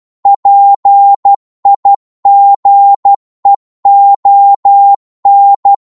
Morse code, heard as a keyed tone, about 12 wpm, 800 Hz. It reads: PIGEON